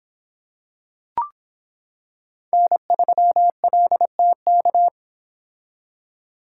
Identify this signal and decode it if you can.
Morse code, heard as a keyed tone, about 26 wpm, 700 Hz. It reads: N3LTK